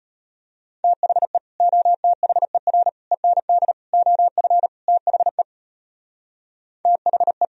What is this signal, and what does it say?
Morse code, heard as a keyed tone, about 31 wpm, 700 Hz. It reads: THE OTHER END OF THE THE
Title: Speed-Racing - Sets of 5 Words 25wpm